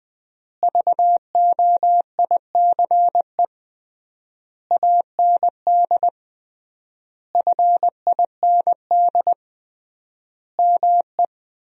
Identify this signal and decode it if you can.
Morse code, heard as a keyed tone, about 20 wpm, 700 Hz. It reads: VOICE AND FIND ME
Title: 3x Word-Spacing - Sentences from Top 500 Words - 20wpm